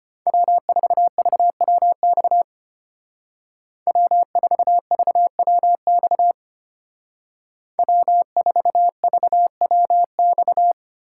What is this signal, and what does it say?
Morse code, heard as a keyed tone, about 32 wpm, 700 Hz. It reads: W4VWX W4VWX W4VWX